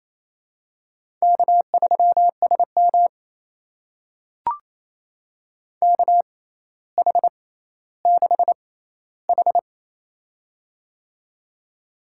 Morse code, heard as a keyed tone, about 27 wpm, spaced as for 14 wpm, 700 Hz. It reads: K3SM KH6H
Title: USA Call Signs - 4 Characters - 6x Character Spacing - 28wpm